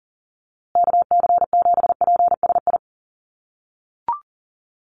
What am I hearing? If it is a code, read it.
Morse code, heard as a keyed tone, about 40 wpm, 700 Hz. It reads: KC7PSI
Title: USA Call Signs Repeated 3x 40wpm